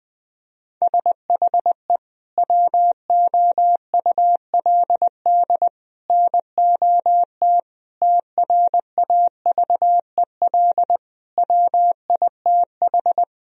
Morse code, heard as a keyed tone, about 20 wpm, 700 Hz. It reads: SHE WOULD NOT TRAVEL WITH
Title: Sentences from Top 1000 Words - Review Part 2 - 20wpm